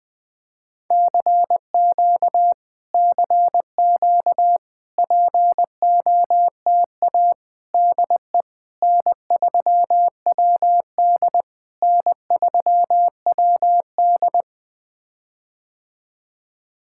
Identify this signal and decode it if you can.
Morse code, heard as a keyed tone, about 20 wpm, 700 Hz. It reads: CQ CQ POTA DE N3WD N3WD
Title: POTA QSO Practice - 20wpm